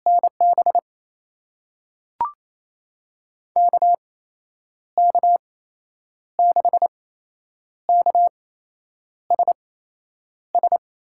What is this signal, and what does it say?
Morse code, heard as a keyed tone, about 28 wpm, spaced as for 9 wpm, 700 Hz. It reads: NB KK6KSS
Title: USA Call Signs - 6 Characters - 8x Character Spacing - 28wpm